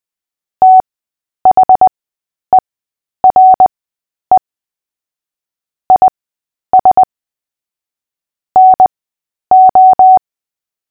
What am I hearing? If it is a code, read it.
Morse code, heard as a keyed tone, about 20 wpm, spaced as for 10 wpm, 750 Hz. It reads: THERE IS NO
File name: cw / fists / Short-Story-8-10wpm